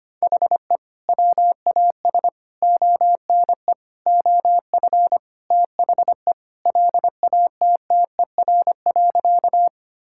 Morse code, heard as a keyed tone, about 25 wpm, 700 Hz. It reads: HE WAS ONE OF THE LATTER.
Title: Sentences from Top 1600 Words 25wpm